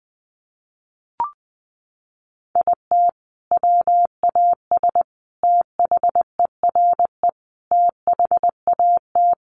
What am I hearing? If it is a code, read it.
Morse code, heard as a keyed tone, about 20 wpm, 700 Hz. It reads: IT WAS THERE THAT
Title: Sentences from Top 1500 Words 20wpm